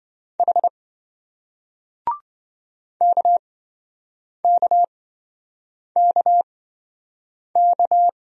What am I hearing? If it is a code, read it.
Morse code, heard as a keyed tone, about 27 wpm, spaced as for 7 wpm, 700 Hz. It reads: H KKKK